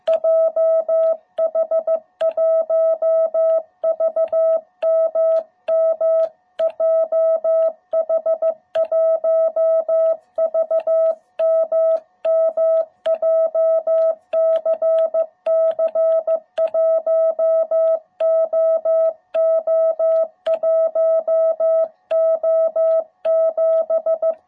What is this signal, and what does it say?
Morse code, heard as a keyed tone, about 15 wpm, 660 Hz. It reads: JH1VMM JH1VMM JCC 1OO1O7